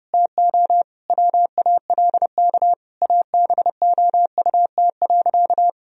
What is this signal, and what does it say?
Morse code, heard as a keyed tone, about 30 wpm, 700 Hz. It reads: TO WALK ABOUT.
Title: Sentences from Top 1000 Words - Review Part 1 - 30wpm